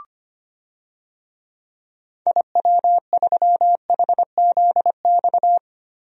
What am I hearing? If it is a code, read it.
Morse code, heard as a keyed tone, about 25 wpm, 700 Hz. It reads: IW3HZX
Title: DXCC-100 Call Signs - 6 Characters - 25wpm